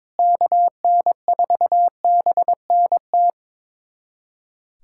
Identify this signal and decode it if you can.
Morse code, heard as a keyed tone, about 22 wpm, 700 Hz. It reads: KN4BNT
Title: USA Callsign DE Callsign - 6 Characters - 22wpm